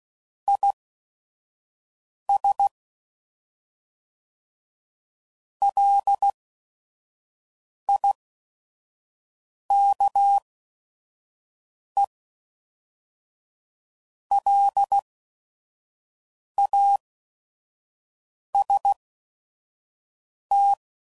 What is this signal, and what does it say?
Morse code, heard as a keyed tone, about 16 wpm, spaced as for 5 wpm, 800 Hz. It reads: IS LIKE LAST
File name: CW Code Groups @5WPM / Weather Sets / Weather 4 17